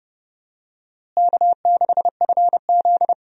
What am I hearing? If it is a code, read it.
Morse code, heard as a keyed tone, about 29 wpm, 700 Hz. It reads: K6FZ
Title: USA Call Signs - 4 Characters - Repeated 2x - 30wpm